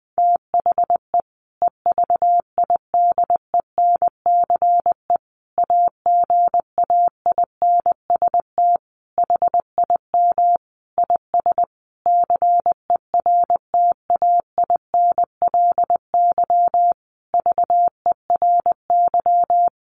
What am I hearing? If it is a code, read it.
Morse code, heard as a keyed tone, about 20 wpm, 700 Hz. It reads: THE EVIDENCE AGAINST HIM IS CERTAINLY VERY
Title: Sentences from Top 1300 Words 20wpm